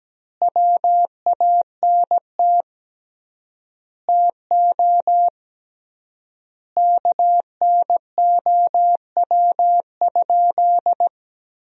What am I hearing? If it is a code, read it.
Morse code, heard as a keyed tone, about 17 wpm, 700 Hz. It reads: WANT TO KNOW?
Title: 3x Word-Spacing - Sentences from Top 200 Words - 17wpm